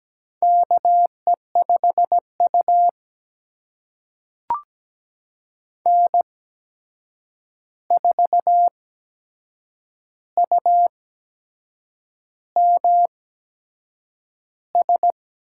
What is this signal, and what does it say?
Morse code, heard as a keyed tone, about 17 wpm, spaced as for 7 wpm, 700 Hz. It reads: KE5U N4UMS
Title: USA Call Signs - 8x Character Spacing - 17wpm